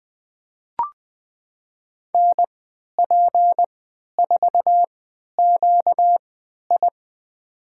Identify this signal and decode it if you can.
Morse code, heard as a keyed tone, about 21 wpm, spaced as for 11 wpm, 700 Hz. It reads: NP4QI
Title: USA Call Signs - 5 Characters - 3x Character Spacing - 20wpm